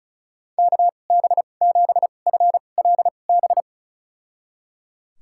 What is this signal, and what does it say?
Morse code, heard as a keyed tone, about 34 wpm, spaced as for 25 wpm, 700 Hz. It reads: KB7FLB
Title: USA Call Signs - 2x Character Spacing - 35wpm